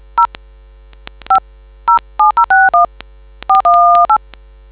DTMF, heard as telephone touch tones, tones of about 130 ms. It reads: 0 5 07061 418